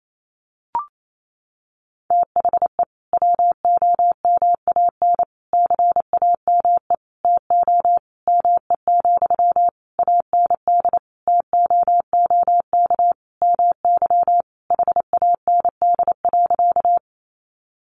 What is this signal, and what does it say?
Morse code, heard as a keyed tone, about 28 wpm, 700 Hz. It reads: THE WOMAN CAME TO ME, AND TOOK MY HAND.
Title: Sentences from Top 1100 Words 28wpm